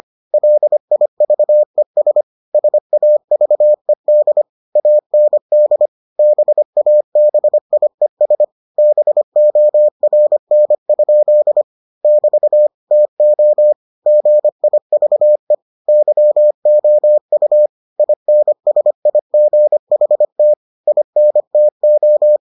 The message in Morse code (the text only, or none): LIVES SAVED AND BABIES BORN? <BT> TO GIVE YOU INSIGHT INTO